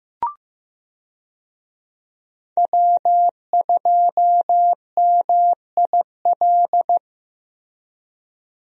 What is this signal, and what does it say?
Morse code, heard as a keyed tone, about 15 wpm, 700 Hz. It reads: W2MIL
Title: USA Call Signs - 5 Characters - Repeated 2x - 15wpm